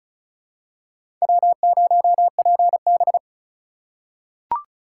Morse code, heard as a keyed tone, about 36 wpm, 700 Hz. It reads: W0PB
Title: USA Call Signs - 4 Characters - 3x Character Spacing - 35wpm